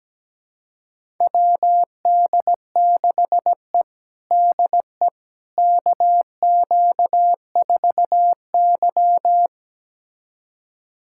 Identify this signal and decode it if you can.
Morse code, heard as a keyed tone, about 17 wpm, 700 Hz. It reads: WD6E DE KQ4Y
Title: USA Callsign DE Callsign - 4 Characters - 17wpm